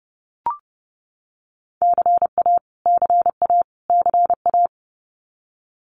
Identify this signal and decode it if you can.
Morse code, heard as a keyed tone, about 30 wpm, 700 Hz. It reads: CA CA CA
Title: US States Repeated 3x 30wpm